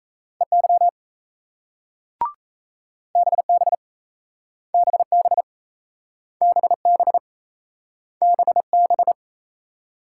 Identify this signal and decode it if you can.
Morse code, heard as a keyed tone, about 38 wpm, 700 Hz. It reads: EY BB BB BB BB